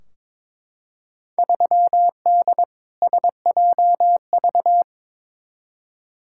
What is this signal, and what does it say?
Morse code, heard as a keyed tone, about 22 wpm, 700 Hz. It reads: 3D SJV
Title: Field Day Exchanges - 22wpm